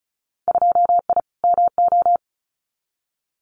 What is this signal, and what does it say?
Morse code, heard as a keyed tone, about 34 wpm, 700 Hz. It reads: 2I MO
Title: Winter Field Day Exchanges 35wpm